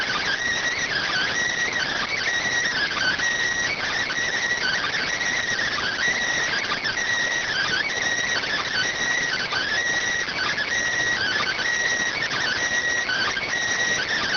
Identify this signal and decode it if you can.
Emergency Alert System (SAME) tones (header bursts), heard as harsh buzzing bursts. It is heard from 0.8 to 13.5 s.